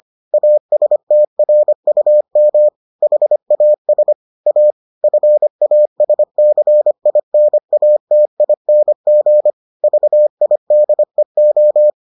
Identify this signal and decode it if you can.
Morse code, heard as a keyed tone, about 25 wpm, 600 Hz. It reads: ASTRUM HAS A FASCINATING VIDEO